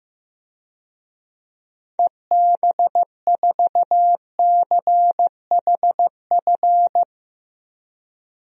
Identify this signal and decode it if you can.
Morse code, heard as a keyed tone, about 16 wpm, 700 Hz. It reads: EB4CHF
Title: DXCC-100 Call Signs - 6 Characters - 15wpm